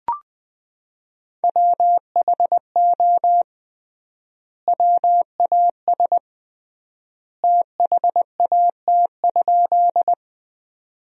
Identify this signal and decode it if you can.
Morse code, heard as a keyed tone, about 20 wpm, 700 Hz. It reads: WHO WAS THAT?
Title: 3x Word-Spacing - Sentences from Top 100 Words - 20wpm